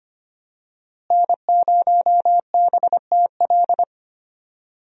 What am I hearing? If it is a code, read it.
Morse code, heard as a keyed tone, about 25 wpm, 700 Hz. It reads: N0BTL
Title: USA Call Signs - 5 Characters - Repeated 3x - 25wpm